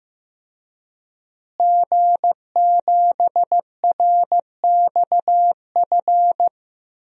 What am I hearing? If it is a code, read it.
Morse code, heard as a keyed tone, about 15 wpm, 700 Hz. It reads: G7RXF
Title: DXCC-100 Call Signs - 5 Characters - Repeated 2x - 15wpm